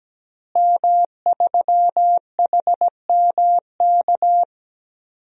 Morse code, heard as a keyed tone, about 17 wpm, 700 Hz. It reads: M3HMK
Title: DXCC-100 Call Signs - 5 Characters - Repeated 2x - 17wpm